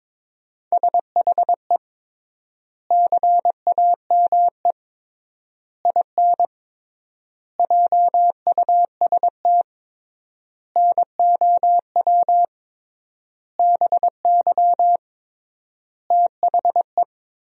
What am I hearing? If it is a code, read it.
Morse code, heard as a keyed tone, about 22 wpm, 700 Hz. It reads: SHE CAME IN JUST NOW BY THE
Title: 3x Word-Spacing - Sentences from Top 300 Words - 22wpm